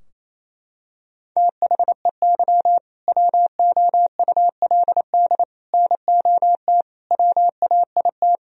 Morse code, heard as a keyed tone, about 28 wpm, 700 Hz. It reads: THEY WOULD NOT WAIT
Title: Sentences from Top 1000 Words - Review Part 3 - 28wpm